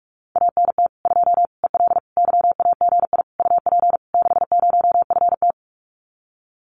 Morse code, heard as a keyed tone, about 44 wpm, 700 Hz. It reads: ANT 2 EL YAGI UP 60FT
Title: Common QSO Phrases Encore 45wpm